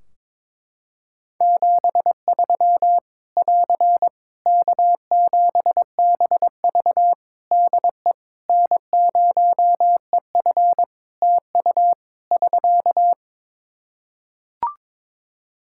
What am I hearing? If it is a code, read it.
Morse code, heard as a keyed tone, about 22 wpm, 700 Hz. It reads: 73 <AR> K7BV DE N0EF TU <SK>
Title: Practice QSO - Exchange 4 - Ending - 22wpm